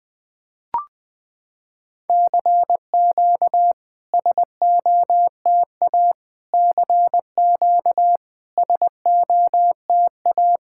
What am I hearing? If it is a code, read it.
Morse code, heard as a keyed tone, about 20 wpm, 700 Hz. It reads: CQ SOTA CQ SOTA
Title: SOTA QSO Practice 20wpm